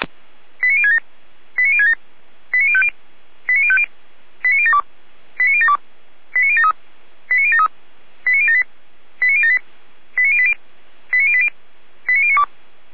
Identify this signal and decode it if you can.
Five-tone selective call (ZVEI), heard as quick runs of five tones, tones of about 70 ms. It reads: 89067 89067 89050 89050 89072 89072 89073 89073 89078 89078 89080 89080 89092
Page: ZVEI